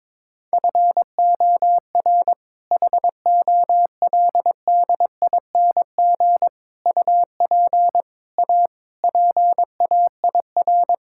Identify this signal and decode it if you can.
Morse code, heard as a keyed tone, about 23 wpm, 700 Hz. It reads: FOR HOLDING UP A PAIR
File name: All-Age Appropriate Jokes - 22wpm